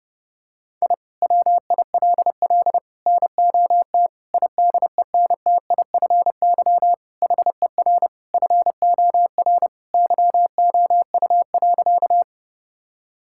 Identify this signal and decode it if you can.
Morse code, heard as a keyed tone, about 30 wpm, 700 Hz. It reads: I WILL NOT IDENTIFY HER FOR YOU.
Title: Sentences from Top 1000 Words - Review Part 6 - 30wpm